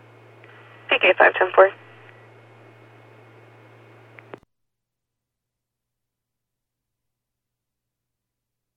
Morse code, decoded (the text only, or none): none